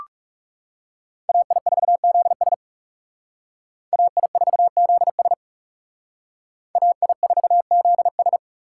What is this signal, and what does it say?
Morse code, heard as a keyed tone, about 36 wpm, 700 Hz. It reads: AI4ZS AI4ZS AI4ZS
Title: Speed Racing - USA 2x2 Call Signs 30wpm